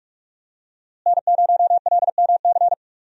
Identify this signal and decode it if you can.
Morse code, heard as a keyed tone, about 45 wpm, 700 Hz. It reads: N0LMC
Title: USA Call Signs - 5 Characters - Repeated 3x - 45wpm